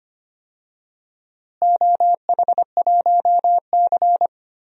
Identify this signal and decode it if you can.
Morse code, heard as a keyed tone, about 25 wpm, 700 Hz. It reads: OH1C
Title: DXCC-100 Call Signs - 4 Characters - Repeated 3x - 25wpm